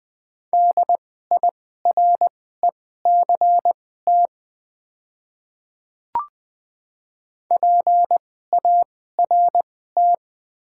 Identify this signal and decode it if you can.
Morse code, heard as a keyed tone, about 21 wpm, spaced as for 15 wpm, 690 Hz. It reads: DIRECT PART
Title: 2x Character Spacing - Top 500 Words - 20wpm